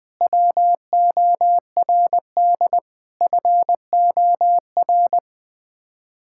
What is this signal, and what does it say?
Morse code, heard as a keyed tone, about 20 wpm, 700 Hz. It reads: WORD FOR
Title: Sets of 8 Words - 20wpm